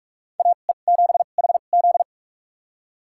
Morse code, heard as a keyed tone, about 46 wpm, spaced as for 32 wpm, 700 Hz. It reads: AE7HZ
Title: USA Call Signs - 5 Characters - 2x Character Spacing - 45wpm